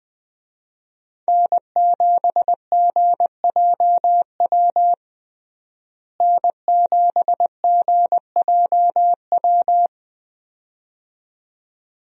Morse code, heard as a keyed tone, about 20 wpm, 700 Hz. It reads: N7GJW N7GJW
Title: USA Call Signs - 5 Characters - Repeated 2x - 20wpm